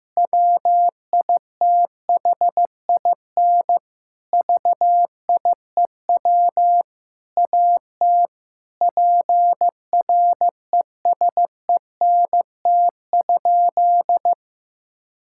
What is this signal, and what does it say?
Morse code, heard as a keyed tone, about 15 wpm, 700 Hz. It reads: WITHIN VIEW AT PRESENT?
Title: Sentences from Top 1000 Words - Review Part 6 - 15wpm